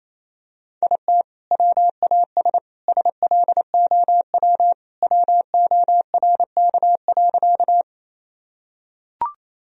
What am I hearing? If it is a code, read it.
Morse code, heard as a keyed tone, about 27 wpm, 700 Hz. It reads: IT WAS SLOW WORK.
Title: Sentences from Top 1000 Words - Review Part 2 - 28wpm